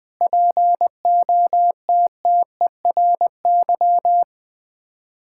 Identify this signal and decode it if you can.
Morse code, heard as a keyed tone, about 21 wpm, 700 Hz. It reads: POTTERY
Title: Difficult Words with Hints - 20wpm